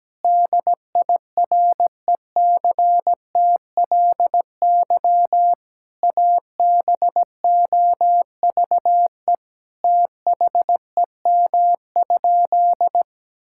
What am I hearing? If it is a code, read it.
Morse code, heard as a keyed tone, about 17 wpm, 700 Hz. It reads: DIRECTLY ABOVE THEM?
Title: Sentences from Top 1000 Words - Review Part 5 - 17wpm